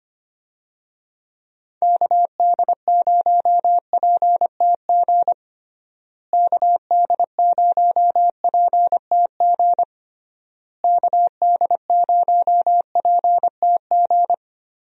Morse code, heard as a keyed tone, about 25 wpm, 700 Hz. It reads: KD0PTG KD0PTG KD0PTG
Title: USA Call Signs - 6 Characters - Repeated 3x - 25wpm